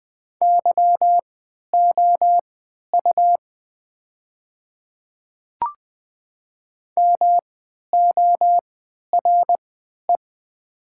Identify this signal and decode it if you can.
Morse code, heard as a keyed tone, about 20 wpm, spaced as for 11 wpm, 700 Hz. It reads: YOU MORE
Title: 3x Character Spacing - Top 200 Words - 20wpm